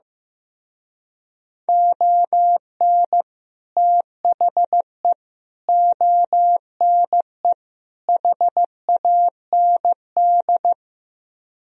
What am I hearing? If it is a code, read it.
Morse code, heard as a keyed tone, about 15 wpm, 700 Hz. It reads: ON THE ONE HAND
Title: Speed-Racing - Sets of 4 Words 15wpm